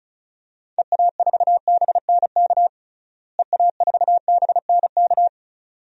Morse code, heard as a keyed tone, about 36 wpm, 700 Hz. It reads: EA4BNK EA4BNK
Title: DXCC-100 Call Signs - 6 Characters - Repeated 2x - 35wpm